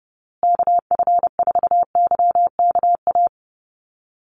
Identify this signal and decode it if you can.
Morse code, heard as a keyed tone, about 28 wpm, 700 Hz. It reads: KF4YKA
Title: USA Call Signs Repeated 2x 30wpm